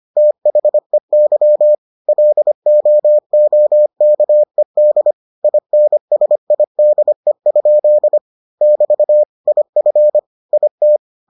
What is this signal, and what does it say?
Morse code, heard as a keyed tone, about 25 wpm, 600 Hz. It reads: THEY LOOKED INSIDE? <BT> IF IT